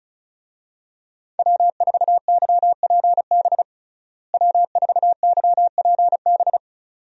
Morse code, heard as a keyed tone, about 36 wpm, 700 Hz. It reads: W4YPB W4YPB
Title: USA Call Signs - 5 Characters - Repeated 2x - 35wpm